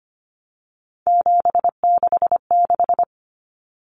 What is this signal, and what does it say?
Morse code, heard as a keyed tone, about 25 wpm, 700 Hz. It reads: 766
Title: Common Cut Numbers - 3-Digit 25wpm